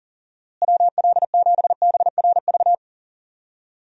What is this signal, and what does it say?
Morse code, heard as a keyed tone, about 37 wpm, 700 Hz. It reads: WL7BRV
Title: USA Call Signs - 6 Characters - Repeated 2x - 40wpm